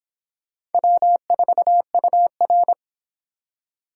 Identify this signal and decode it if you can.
Morse code, heard as a keyed tone, about 30 wpm, 700 Hz. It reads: W4UR